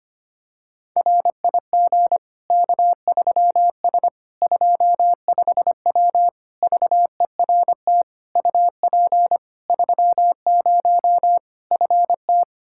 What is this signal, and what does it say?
Morse code, heard as a keyed tone, about 25 wpm, 700 Hz. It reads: RIG K3S 25W VERT UP 30 FT
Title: Practice QSO - Exchange 2 - Info - 25wpm